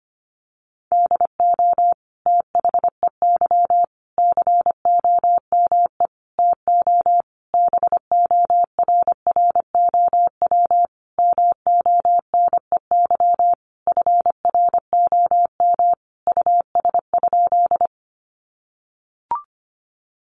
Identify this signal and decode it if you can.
Morse code, heard as a keyed tone, about 25 wpm, 700 Hz. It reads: DO THEY COME TO BORROW MONEY FROM US?
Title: Sentences from Top 1500 Words 25wpm